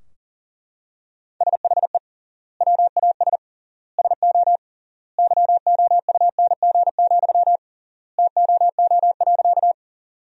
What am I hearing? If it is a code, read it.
Morse code, heard as a keyed tone, about 40 wpm, 700 Hz. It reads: SHE WAS SO YOUNG, TOO.
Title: 3x Word-Spacing - Sentences from Top 400 Words - 40wpm